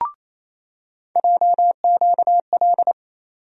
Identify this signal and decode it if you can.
Morse code, heard as a keyed tone, about 28 wpm, 700 Hz. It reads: JQL
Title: USA Call Sign Postfixes - 3 Letters 28wpm